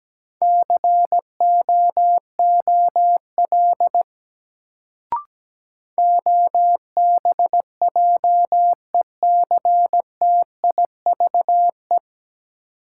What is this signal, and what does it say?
Morse code, heard as a keyed tone, about 17 wpm, 700 Hz. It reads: COOL OBJECTIVE
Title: Sentences from Top 1200 Words - New Word Drill - 17wpm